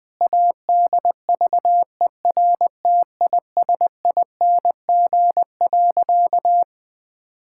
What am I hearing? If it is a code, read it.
Morse code, heard as a keyed tone, about 21 wpm, 700 Hz. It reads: ADVERTISING.
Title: Sentences from Top 1000 Words - Review Part 6 - 20wpm